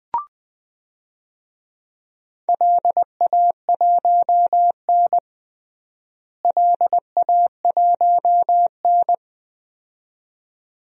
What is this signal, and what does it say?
Morse code, heard as a keyed tone, about 21 wpm, 700 Hz. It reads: LA1N LA1N
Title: DXCC-100 Call Signs - 4 Characters - Repeated 2x - 20wpm